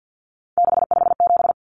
Morse code, heard as a keyed tone, about 50 wpm, 700 Hz. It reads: 657
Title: Common Cut Numbers - 3-Digit 50wpm